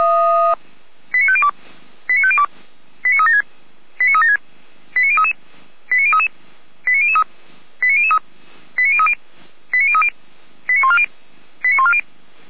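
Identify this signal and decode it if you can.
Five-tone selective call (ZVEI), heard as quick runs of five tones, tones of about 70 ms. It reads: 89502 89502 89376 89376 89033 89033 89003 89003 89030 89030 89140 89140
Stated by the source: ZVEI five-tone selective call received by SDR.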